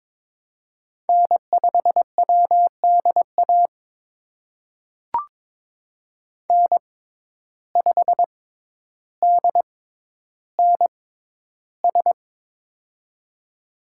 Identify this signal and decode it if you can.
Morse code, heard as a keyed tone, about 22 wpm, spaced as for 11 wpm, 700 Hz. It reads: N5WDA N5DNS
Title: USA Call Signs - 5 Characters - 6x Character Spacing - 22wpm